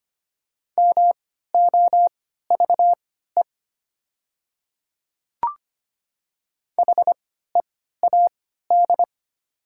Morse code, heard as a keyed tone, about 25 wpm, spaced as for 14 wpm, 700 Hz. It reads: MOVE HEAD
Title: 3x Character Spacing - Top 300 Words - 25wpm